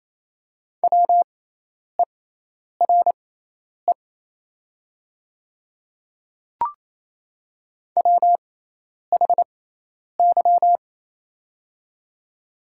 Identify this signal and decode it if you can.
Morse code, heard as a keyed tone, about 28 wpm, spaced as for 10 wpm, 700 Hz. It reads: WERE WHY
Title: 6x Character Spacing - Top 200 Words - 28wpm